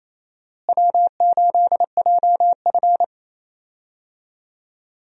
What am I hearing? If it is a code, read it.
Morse code, heard as a keyed tone, about 28 wpm, 700 Hz. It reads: W8JF
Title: USA Call Signs - 4 Characters - Repeated 2x - 28wpm